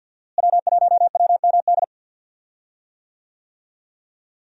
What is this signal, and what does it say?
Morse code, heard as a keyed tone, about 48 wpm, 700 Hz. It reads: W1WMD
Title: USA Call Signs - 5 Characters - Repeated 3x - 50wpm